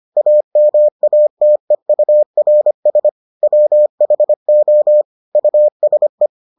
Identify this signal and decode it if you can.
Morse code, heard as a keyed tone, about 25 wpm, 600 Hz. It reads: AMATEURS WHO USE